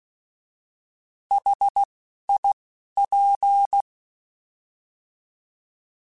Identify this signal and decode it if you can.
Morse code, heard as a keyed tone, about 16 wpm, 800 Hz. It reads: H I P
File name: Contrasts / Contrasts-mp3s / 3LtrContrasts @10